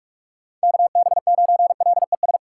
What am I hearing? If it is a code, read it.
Morse code, heard as a keyed tone, about 41 wpm, 700 Hz. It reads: KB9LES
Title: USA Call Signs - 6 Characters - 4x Character Spacing - 45wpm